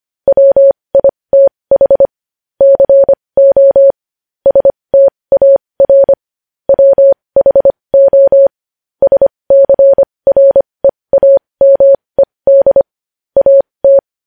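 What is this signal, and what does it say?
Morse code, heard as a keyed tone, about 25 wpm, spaced as for 20 wpm, 550 Hz. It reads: WITH CO STAR WHO SCREAMED AT
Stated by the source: Morse code sending news headlines at 20 wpm.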